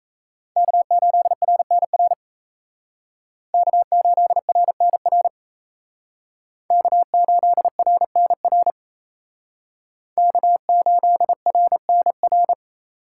Morse code, heard as a keyed tone, about 34 wpm, 700 Hz. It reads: K8RNR K8RNR K8RNR K8RNR
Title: Speed Racing - USA 1x3 Call Signs 28wpm